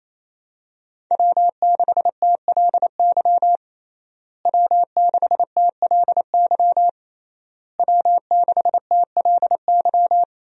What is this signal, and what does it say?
Morse code, heard as a keyed tone, about 28 wpm, 700 Hz. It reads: W6TLY W6TLY W6TLY
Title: USA Call Signs - 5 Characters - Repeated 3x - 28wpm